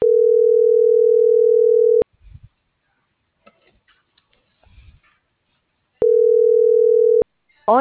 Ringback tone (North America), 440 + 480 Hz.